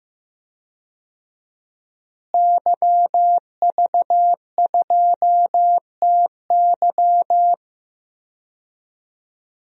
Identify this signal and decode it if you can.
Morse code, heard as a keyed tone, about 15 wpm, 700 Hz. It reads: YV2TY
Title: DXCC-100 Call Signs - 5 Characters - Repeated 3x - 15wpm